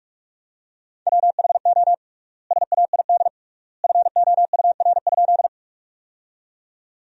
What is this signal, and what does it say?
Morse code, heard as a keyed tone, about 42 wpm, 700 Hz. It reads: WHO SAID FOUR?
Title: 3x Word-Spacing - Sentences from Top 300 Words - 45wpm